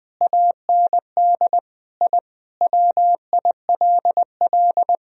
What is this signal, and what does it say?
Morse code, heard as a keyed tone, about 20 wpm, 700 Hz. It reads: AND I WILL
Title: Sentences from Top 1000 Words - Review Part 2 - 20wpm